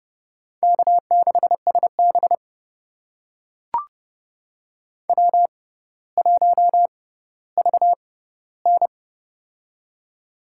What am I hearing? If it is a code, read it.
Morse code, heard as a keyed tone, about 31 wpm, spaced as for 16 wpm, 700 Hz. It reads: K6SB W1VN
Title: USA Call Signs - 4 Characters - 6x Character Spacing - 30wpm